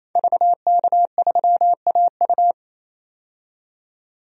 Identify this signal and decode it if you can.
Morse code, heard as a keyed tone, about 28 wpm, 700 Hz. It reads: VK3AU
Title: DXCC-100 Call Signs - 5 Characters - 28wpm